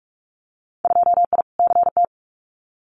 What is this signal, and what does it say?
Morse code, heard as a keyed tone, about 46 wpm, 700 Hz. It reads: 2I CT
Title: Winter Field Day Exchanges 45wpm